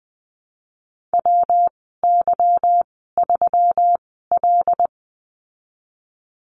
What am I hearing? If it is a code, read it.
Morse code, heard as a keyed tone, about 20 wpm, 700 Hz. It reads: W Y 3 L
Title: 154 Lesson 25g - Letter p - Call Signs 20wpm